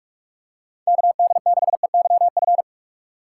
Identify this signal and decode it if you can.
Morse code, heard as a keyed tone, about 42 wpm, 700 Hz. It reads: KD6EYF
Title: USA Call Signs - 4x Character Spacing - 45wpm